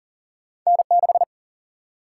Morse code, heard as a keyed tone, about 40 wpm, 700 Hz. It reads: N6
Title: Single-Letter Single-Number 40wpm